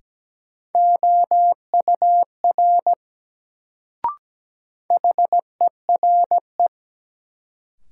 Morse code, heard as a keyed tone, about 17 wpm, 700 Hz. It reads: OUR HERE
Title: Sentences from Top 200 Words - New Word Drill - 17wpm